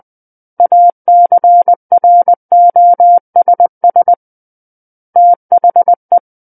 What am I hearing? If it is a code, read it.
Morse code, heard as a keyed tone, about 20 wpm, 700 Hz. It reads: ACROSS THE
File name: 20 WPM